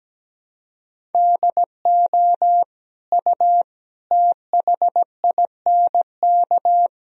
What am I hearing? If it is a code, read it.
Morse code, heard as a keyed tone, about 17 wpm, 700 Hz. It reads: DO U THINK
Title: Sentences from Top 200 Words and Abbreviations 17wpm